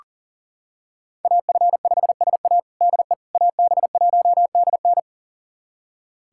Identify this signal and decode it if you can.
Morse code, heard as a keyed tone, about 40 wpm, 700 Hz. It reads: AF5SA DE AB1DN
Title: USA Callsign DE Callsign - 5 Characters - 40wpm